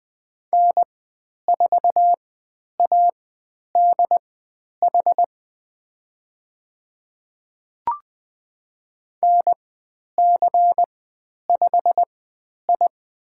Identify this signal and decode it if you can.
Morse code, heard as a keyed tone, about 20 wpm, spaced as for 10 wpm, 700 Hz. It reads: N4ADH NC5I
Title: 109 Lesson 18g - Letter m - Call Signs 20wpm